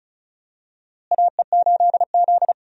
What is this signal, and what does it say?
Morse code, heard as a keyed tone, about 36 wpm, 700 Hz. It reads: AE8Z
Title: USA Call Signs - 4 Characters - Repeated 2x - 35wpm